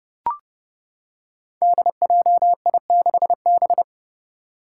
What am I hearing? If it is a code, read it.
Morse code, heard as a keyed tone, about 30 wpm, 700 Hz. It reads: DJI6B
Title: Five Challenging Letter-Number Combinations 30wpm